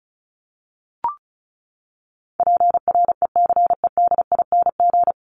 Morse code, heard as a keyed tone, about 34 wpm, 700 Hz. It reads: PRECEDING
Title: ING Suffix - Top 100 Words with Base Words 35wpm